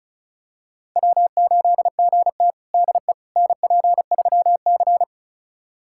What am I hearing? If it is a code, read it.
Morse code, heard as a keyed tone, about 36 wpm, 700 Hz. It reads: W8GT DE NP3C
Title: USA Callsign DE Callsign - 4 Characters - 35wpm